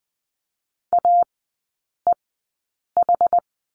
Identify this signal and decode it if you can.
Morse code, heard as a keyed tone, about 20 wpm, spaced as for 8 wpm, 700 Hz. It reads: AEH
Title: 060 Lesson 11e - Letter h - Three Characters 20wpm